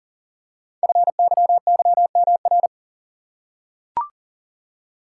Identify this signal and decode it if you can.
Morse code, heard as a keyed tone, about 40 wpm, 700 Hz. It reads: FYYMR
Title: Code Groups - 5 Letter:Numbers - 40wpm